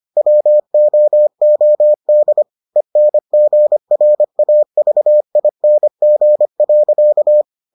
Morse code, heard as a keyed tone, about 25 wpm, 600 Hz. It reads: WOOD ENGRAVING.